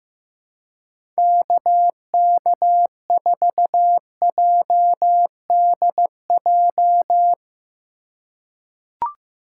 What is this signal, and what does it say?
Morse code, heard as a keyed tone, about 15 wpm, 700 Hz. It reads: KK4JDJ
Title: USA Call Signs - 6 Characters - 3x Character Spacing - 15wpm